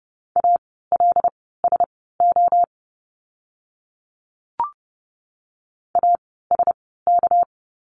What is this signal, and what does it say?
Morse code, heard as a keyed tone, about 30 wpm, spaced as for 17 wpm, 700 Hz. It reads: ALSO ASK
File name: course / 30:10wpm / 230 Lesson 37f - Letter x - Words 30wpm